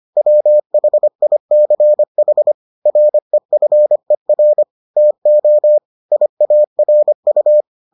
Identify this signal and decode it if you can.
Morse code, heard as a keyed tone, about 25 wpm, 600 Hz. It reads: WHICH REFER TO IARU